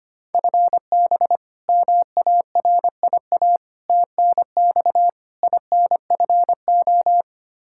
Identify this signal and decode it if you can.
Morse code, heard as a keyed tone, about 25 wpm, 700 Hz. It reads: FB MARIA TNX INFO
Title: K1USN SST QSO Practice 25wpm